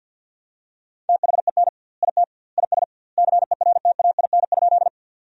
Morse code, heard as a keyed tone, about 50 wpm, 700 Hz. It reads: THEN IT IS CERTAIN?
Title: 2x Word-Spacing - Sentences from Top 500 Words - 50wpm